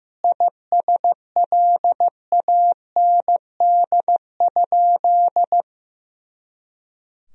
Morse code, heard as a keyed tone, about 15 wpm, 700 Hz. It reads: ISLAND?
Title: Sentences from Top 500 Words and Abbreviations 15wpm